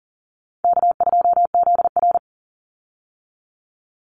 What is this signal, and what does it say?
Morse code, heard as a keyed tone, about 40 wpm, 700 Hz. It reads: K2ZR
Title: Easy Call Signs - 40wpm